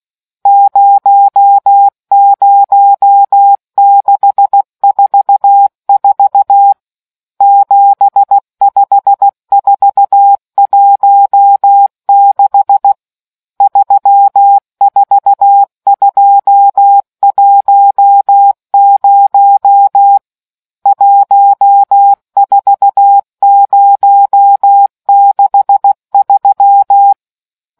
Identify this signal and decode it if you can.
Morse code, heard as a keyed tone, about 16 wpm, 800 Hz. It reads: 00644 75416 34210 14063